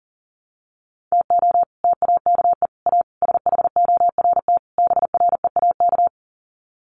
Morse code, heard as a keyed tone, about 40 wpm, 700 Hz. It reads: TO TAKE A SHORT BREAK
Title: Sets of 5 Words Encore2 40wpm